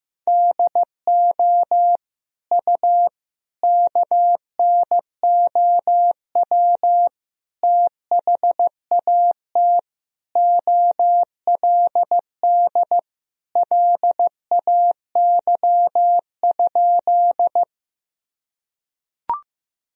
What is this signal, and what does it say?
Morse code, heard as a keyed tone, about 15 wpm, 700 Hz. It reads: DO U KNOW THAT OLD LAY?
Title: Sentences from Top 400 Words and Abbreviations 15wpm